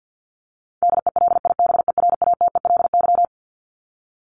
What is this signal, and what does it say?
Morse code, heard as a keyed tone, about 50 wpm, 700 Hz. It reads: DELIBERATELY
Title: Top 1751-2000 Words 50wpm